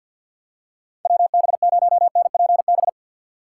Morse code, heard as a keyed tone, about 52 wpm, 700 Hz. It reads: WB0NPB
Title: USA Call Signs - 4x Character Spacing - 50wpm